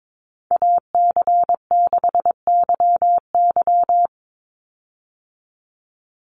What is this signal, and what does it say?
Morse code, heard as a keyed tone, about 22 wpm, 700 Hz. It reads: AC6YY
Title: Call Signs - 22wpm